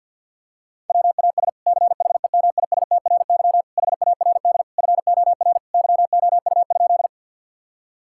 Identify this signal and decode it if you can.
Morse code, heard as a keyed tone, about 50 wpm, 700 Hz. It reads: WAS CHEMISTRY HARD FOR YOU?
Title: Sentences from Top 1000 Words - Review Part 6 - 50wpm